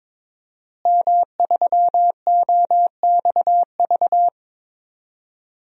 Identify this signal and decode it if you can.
Morse code, heard as a keyed tone, about 22 wpm, 700 Hz. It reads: M3OXV
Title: DXCC-100 Call Signs - 5 Characters - Repeated 3x - 22wpm